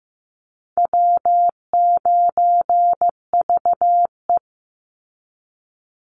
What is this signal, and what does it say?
Morse code, heard as a keyed tone, about 15 wpm, 700 Hz. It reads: W9VE
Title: Call Signs Repeated 2x - 15wpm